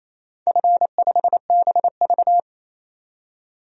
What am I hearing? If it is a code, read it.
Morse code, heard as a keyed tone, about 28 wpm, 700 Hz. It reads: F5BV
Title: DXCC-100 Call Signs - 4 Characters - 28wpm